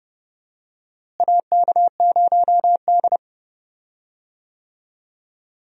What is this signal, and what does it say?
Morse code, heard as a keyed tone, about 29 wpm, 700 Hz. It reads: AK0D
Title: USA Call Signs - 4 Characters - Repeated 3x - 30wpm